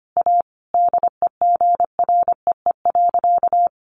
Morse code, heard as a keyed tone, about 25 wpm, 700 Hz. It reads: A DEGREE.
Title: Sentences from Top 1500 Words 25wpm